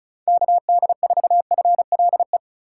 Morse code, heard as a keyed tone, about 36 wpm, 700 Hz. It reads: KD4FLE
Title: USA Call Signs - 6 Characters - 2x Character Spacing - 35wpm